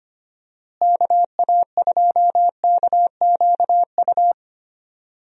Morse code, heard as a keyed tone, about 25 wpm, 700 Hz. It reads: KA2KQU
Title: USA Call Signs - 6 Characters - Repeated 2x - 25wpm